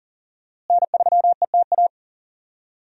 Morse code, heard as a keyed tone, about 40 wpm, 700 Hz. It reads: N3ETA
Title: USA Call Signs - 5 Characters - Repeated 2x - 40wpm